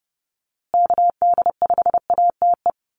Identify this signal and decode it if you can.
Morse code, heard as a keyed tone, about 30 wpm, 700 Hz. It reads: KD5ATE
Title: USA Call Signs Repeated 2x 30wpm